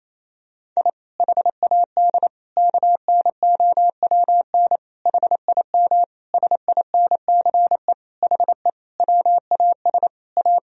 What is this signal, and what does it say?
Morse code, heard as a keyed tone, about 28 wpm, 700 Hz. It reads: I HAD KNOWN HIM SINCE HE WAS A
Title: Sentences from Top 1000 Words - Review Part 3 - 28wpm